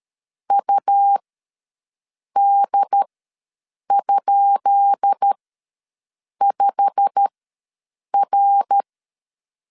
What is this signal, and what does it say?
Morse code, heard as a keyed tone, about 13 wpm, spaced as for 6 wpm, 800 Hz. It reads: UD?5R
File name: LEZIONI / 10 SIGNS / 10B INT ws 12wpm 3527